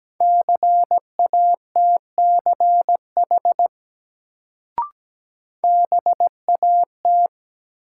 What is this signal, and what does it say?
Morse code, heard as a keyed tone, about 17 wpm, 700 Hz. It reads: CATCH BAT
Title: Sentences from Top 1500 Words - New Word Drill - 17wpm